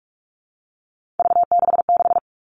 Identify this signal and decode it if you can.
Morse code, heard as a keyed tone, about 44 wpm, 700 Hz. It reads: V66
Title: All Cut Numbers - 3-Digit 45wpm